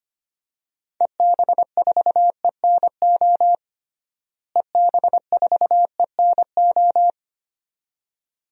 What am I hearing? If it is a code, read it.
Morse code, heard as a keyed tone, about 25 wpm, 700 Hz. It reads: EB4ENO EB4ENO
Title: DXCC-100 Call Signs - 6 Characters - Repeated 2x - 25wpm